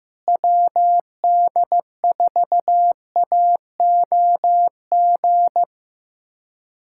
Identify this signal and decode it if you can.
Morse code, heard as a keyed tone, about 15 wpm, 700 Hz. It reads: WD4AOG
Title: USA Callsign DE Callsign 15wpm